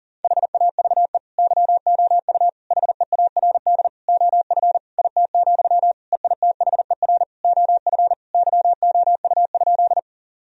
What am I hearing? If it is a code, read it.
Morse code, heard as a keyed tone, about 40 wpm, 700 Hz. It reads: HAVE YOU HEARD OF IT, EITHER OF YOU?
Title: Sentences from Top 1000 Words - Review Part 4 - 40wpm